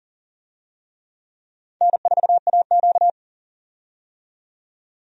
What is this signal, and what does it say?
Morse code, heard as a keyed tone, about 40 wpm, 700 Hz. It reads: N4AQ
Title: USA Call Signs - 4 Characters - 40wpm